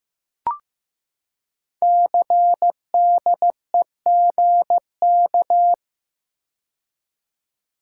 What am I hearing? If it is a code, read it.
Morse code, heard as a keyed tone, about 15 wpm, 700 Hz. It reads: CDEGK
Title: Code Groups - 5 Letters - 15wpm